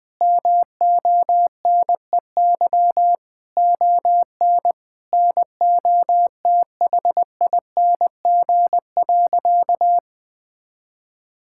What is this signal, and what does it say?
Morse code, heard as a keyed tone, about 20 wpm, 700 Hz. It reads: MONEY ON NOTHING.
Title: Sentences from Top 1000 Words - Review Part 5 - 20wpm